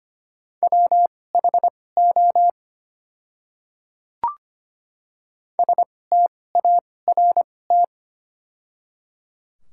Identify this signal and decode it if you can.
Morse code, heard as a keyed tone, about 25 wpm, spaced as for 18 wpm, 700 Hz. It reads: WHO START
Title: 2x Character Spacing - Top 500 Words - 25wpm